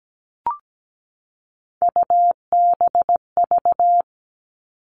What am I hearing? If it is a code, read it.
Morse code, heard as a keyed tone, about 17 wpm, 700 Hz. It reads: UBV
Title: All Cut Numbers - 3-Digit 17wpm